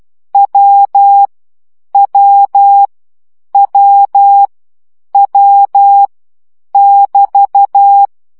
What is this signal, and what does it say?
Morse code, heard as a keyed tone, about 12 wpm, 800 Hz. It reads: W W W W <BT>
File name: CW gyakorlas w 10 baud